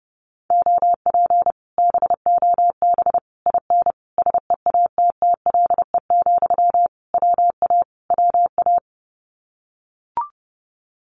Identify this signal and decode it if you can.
Morse code, heard as a keyed tone, about 30 wpm, 700 Hz. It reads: OP BOB IN SEATTLE, WA WA
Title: Common QSO Phrases Encore 30wpm